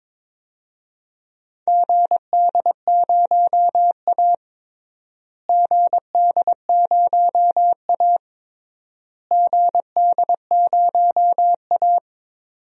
Morse code, heard as a keyed tone, about 22 wpm, 700 Hz. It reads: GD0A GD0A GD0A
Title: DXCC-100 Call Signs - 4 Characters - Repeated 3x - 22wpm